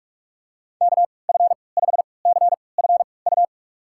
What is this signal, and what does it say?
Morse code, heard as a keyed tone, about 42 wpm, spaced as for 25 wpm, 690 Hz. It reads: KF5CFU
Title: USA Call Signs - 6 Characters - 3x Character Spacing - 45wpm